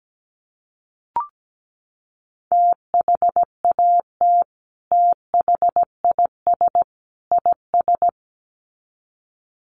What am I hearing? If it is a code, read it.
Morse code, heard as a keyed tone, about 17 wpm, 700 Hz. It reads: THAT THIS IS
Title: Sets of 3 Words 17wpm